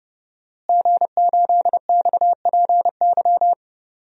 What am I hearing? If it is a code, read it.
Morse code, heard as a keyed tone, about 30 wpm, 700 Hz. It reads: G8XPY
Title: DXCC-100 Call Signs - 5 Characters - Repeated 3x - 30wpm